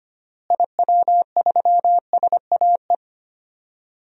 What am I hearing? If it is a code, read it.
Morse code, heard as a keyed tone, about 25 wpm, 700 Hz. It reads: IW3SAE
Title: DXCC-100 Call Signs - 6 Characters - Repeated 3x - 25wpm